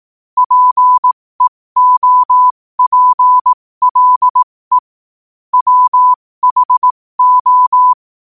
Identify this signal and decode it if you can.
Morse code, heard as a keyed tone, about 18 wpm, spaced as for 15 wpm, 1000 Hz. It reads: PEOPLE WHO
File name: qotdfiles / 15wpm